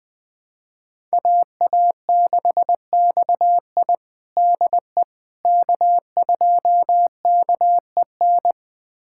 Morse code, heard as a keyed tone, about 20 wpm, 700 Hz. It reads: AA6XI DE K2KEN
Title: USA Callsign DE Callsign - 5 Characters - 20wpm